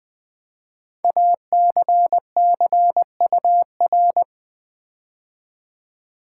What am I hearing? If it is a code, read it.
Morse code, heard as a keyed tone, about 20 wpm, 700 Hz. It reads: ACCUR
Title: Build A Word - 5 Letters At A Time - 10-Letter Words - 20wpm